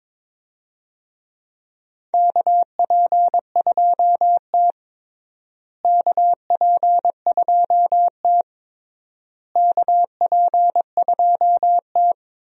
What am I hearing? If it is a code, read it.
Morse code, heard as a keyed tone, about 22 wpm, 700 Hz. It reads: KP2T KP2T KP2T
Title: DXCC-100 Call Signs - 4 Characters - Repeated 3x - 22wpm